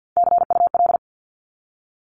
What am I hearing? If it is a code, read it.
Morse code, heard as a keyed tone, about 49 wpm, 700 Hz. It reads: CUL
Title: Abbreviations 50wpm